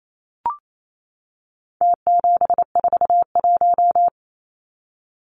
Morse code, heard as a keyed tone, about 28 wpm, 700 Hz. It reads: T741
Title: Numbers - Contest Style - 4-Digit 28wpm